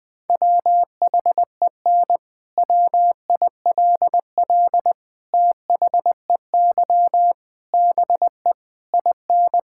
Morse code, heard as a keyed tone, about 21 wpm, 700 Hz. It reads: WHEN WILL THEY BE IN
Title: Sentences from Top 1000 Words - Review Part 6 - 20wpm